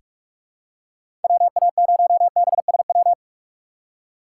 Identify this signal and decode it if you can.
Morse code, heard as a keyed tone, about 46 wpm, 700 Hz. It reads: WA0BSW
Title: USA Call Signs - 6 Characters - 3x Character Spacing - 45wpm